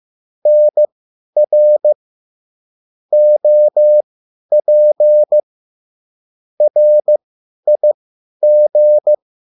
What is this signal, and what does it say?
Morse code, heard as a keyed tone, about 15 wpm, spaced as for 10 wpm, 600 Hz. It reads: NR OP RIG